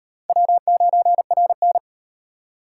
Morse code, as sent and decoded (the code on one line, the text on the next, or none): .-- ----. .-. -.
W9RN